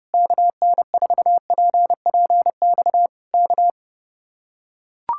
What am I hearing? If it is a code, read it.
Morse code, heard as a keyed tone, about 30 wpm, 700 Hz. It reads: KN4PPX K